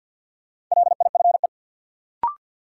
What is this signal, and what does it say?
Morse code, heard as a keyed tone, about 53 wpm, 700 Hz. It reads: LIFE
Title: Sentences from Top 300 Words - New Word Drill - 50wpm